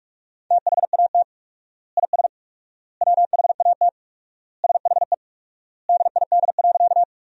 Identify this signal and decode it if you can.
Morse code, heard as a keyed tone, about 46 wpm, 700 Hz. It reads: THAT IS WHAT SHE DID.
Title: 4x Word-Spacing - Sentences from Top 100 Words - 45wpm